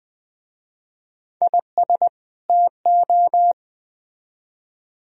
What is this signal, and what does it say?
Morse code, heard as a keyed tone, about 22 wpm, 700 Hz. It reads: IS TO